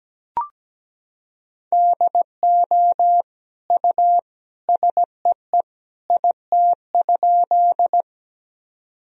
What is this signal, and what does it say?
Morse code, heard as a keyed tone, about 17 wpm, 700 Hz. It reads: DO U SEE IT?
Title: Sentences from Top 100 Words and Abbreviations 17wpm